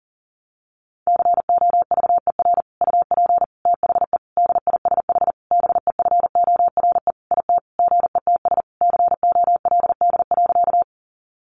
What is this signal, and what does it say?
Morse code, heard as a keyed tone, about 40 wpm, 700 Hz. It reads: COVER UP THE DISH BEFORE IT GETS COLD.
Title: Sentences from Top 1300 Words 40wpm